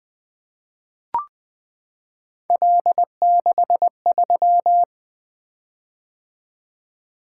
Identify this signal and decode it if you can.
Morse code, heard as a keyed tone, about 20 wpm, 700 Hz. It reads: L63
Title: Three Letter-Number Combinations - 20wpm